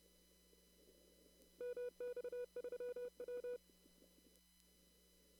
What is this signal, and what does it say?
Morse code, heard as a keyed tone, about 30 wpm, 490 Hz. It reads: MX3W
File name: CW / kt5j wpx cw-op-k5pi